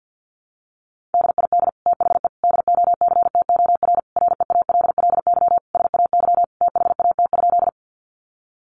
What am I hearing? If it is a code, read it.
Morse code, heard as a keyed tone, about 50 wpm, 700 Hz. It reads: DID THE DOCTOR REALLY SAY THAT?
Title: Sentences from Top 1500 Words 50wpm